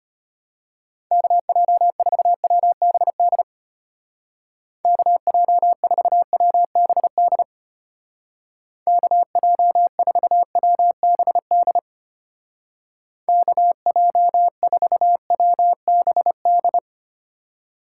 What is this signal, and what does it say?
Morse code, heard as a keyed tone, about 32 wpm, 700 Hz. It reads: KJ4WBD KJ4WBD KJ4WBD KJ4WBD